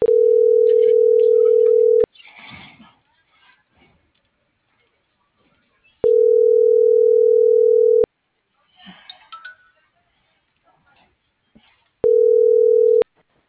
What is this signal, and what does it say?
Ringback tone (North America), 440 + 480 Hz.